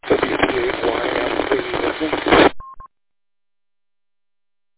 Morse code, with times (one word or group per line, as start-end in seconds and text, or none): none